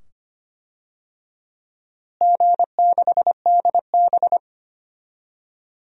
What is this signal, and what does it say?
Morse code, heard as a keyed tone, about 25 wpm, 700 Hz. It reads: G6DB
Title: DXCC-100 Call Signs - 4 Characters - Repeated 2x - 25wpm